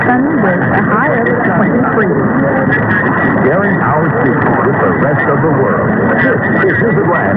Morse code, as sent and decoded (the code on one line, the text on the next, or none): none